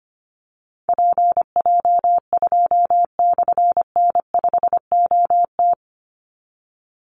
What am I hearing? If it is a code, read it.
Morse code, heard as a keyed tone, about 25 wpm, 700 Hz. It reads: PJ2/N5OT
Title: Call Signs - 25wpm